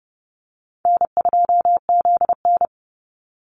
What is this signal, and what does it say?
Morse code, heard as a keyed tone, about 30 wpm, 700 Hz. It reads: N2ZN
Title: Easy Call Signs - 30wpm